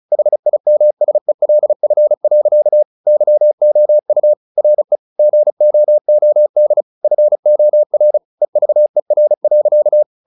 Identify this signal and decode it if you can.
Morse code, heard as a keyed tone, about 34 wpm, 600 Hz. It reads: HIMSELF. YOU RE GOOD FOR EVER.
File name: the old man and the sea 2 35wpm